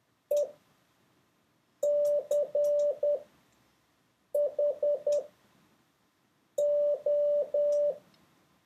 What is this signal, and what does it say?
Morse code, heard as a keyed tone, about 10 wpm, spaced as for 5 wpm, 580 Hz. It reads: ECHO